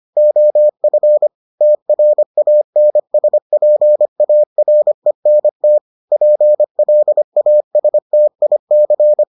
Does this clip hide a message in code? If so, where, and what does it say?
Yes: Morse code from 0.2 to 9.2 s, heard as a keyed tone, about 25 wpm, 600 Hz. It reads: OF TRANSPARENT PLASTIC